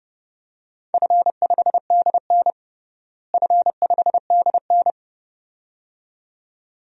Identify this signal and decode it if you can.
Morse code, heard as a keyed tone, about 30 wpm, 700 Hz. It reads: F5DN F5DN
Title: DXCC-100 Call Signs - 4 Characters - Repeated 2x - 30wpm